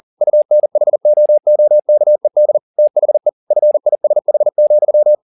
Morse code, heard as a keyed tone, about 40 wpm, 600 Hz. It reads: UNHOOKED THE FISH,
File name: the old man and the sea 2 40wpm